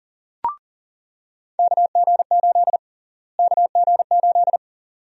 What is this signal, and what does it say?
Morse code, heard as a keyed tone, about 39 wpm, 700 Hz. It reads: KG8 KG8
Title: USA Call Sign Prefixes - Repeated 2x - 40wpm